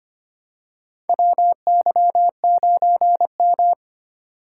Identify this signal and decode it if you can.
Morse code, heard as a keyed tone, about 25 wpm, 700 Hz. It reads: WY9M
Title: USA Call Signs - 4 Characters - 8x Character Spacing - 25wpm